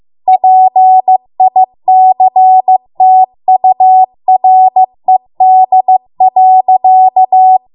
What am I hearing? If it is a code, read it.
Morse code, heard as a keyed tone, about 15 wpm, 750 Hz. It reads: PICTURED.